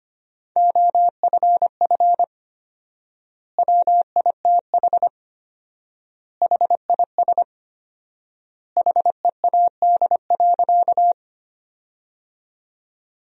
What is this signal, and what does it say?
Morse code, heard as a keyed tone, about 25 wpm, 700 Hz. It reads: OFF WITH HIS HEAD.
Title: 4x Word-Spacing - Sentences from Top 300 Words - 25wpm